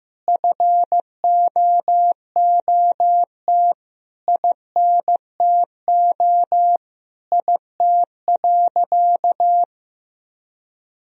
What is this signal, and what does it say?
Morse code, heard as a keyed tone, about 15 wpm, 700 Hz. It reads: FOOT INTO IT.
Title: Sentences from Top 1000 Words - Review Part 3 - 15wpm